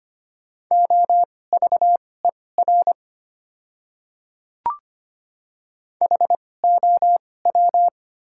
Morse code, heard as a keyed tone, about 25 wpm, spaced as for 18 wpm, 700 Hz. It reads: OVER HOW
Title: 2x Character Spacing - Top 100 Words - 25wpm